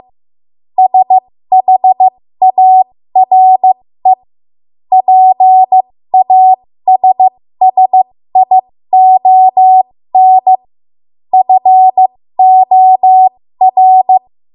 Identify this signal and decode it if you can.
Morse code, heard as a keyed tone, about 15 wpm, spaced as for 13 wpm, 750 Hz. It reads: SHARE PASSION FOR
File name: Morse / 13WPM